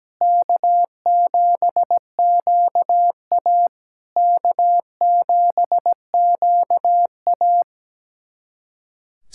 Morse code, heard as a keyed tone, about 17 wpm, 700 Hz. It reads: K7QA K7QA
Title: SOTA QSO Practice 17wpm